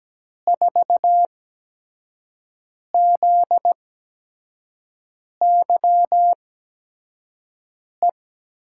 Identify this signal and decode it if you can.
Morse code, heard as a keyed tone, about 17 wpm, spaced as for 5 wpm, 700 Hz. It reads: 4ZYE
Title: USA Call Signs - 6 Characters - 8x Character Spacing - 17wpm